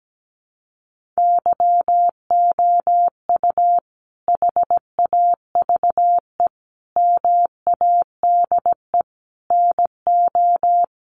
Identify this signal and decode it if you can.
Morse code, heard as a keyed tone, about 17 wpm, 700 Hz. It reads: YOU HAVE MADE NO
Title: Sentences from Top 1400 Words 17wpm